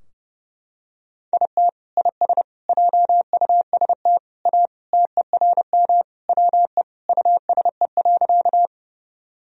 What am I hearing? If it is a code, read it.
Morse code, heard as a keyed tone, about 32 wpm, 700 Hz. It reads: IT IS JUST A TERM WE USE.
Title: Sentences from Top 1000 Words - Review Part 4 - 30wpm